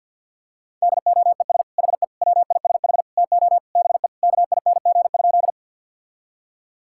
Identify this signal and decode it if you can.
Morse code, heard as a keyed tone, about 51 wpm, 700 Hz. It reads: DOES HE WISH TO BE KING?
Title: Sentences from Top 1600 Words 50wpm